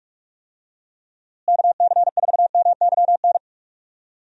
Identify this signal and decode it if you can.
Morse code, heard as a keyed tone, about 46 wpm, 700 Hz. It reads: KC4MYN
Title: USA Call Signs - 6 Characters - Repeated 2x - 45wpm